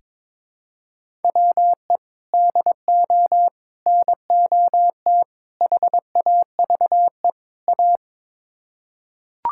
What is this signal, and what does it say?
Morse code, heard as a keyed tone, about 22 wpm, 700 Hz. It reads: WE DO NOT HAVE A
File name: practice / 22wpm / Build A Phrase - 5-Word Phrases - 22wpm